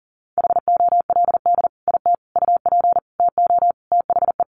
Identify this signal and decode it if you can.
Morse code, heard as a keyed tone, about 37 wpm, 700 Hz. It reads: HOLD IT UP TO THE
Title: Sentences from Top 400 Words 40wpm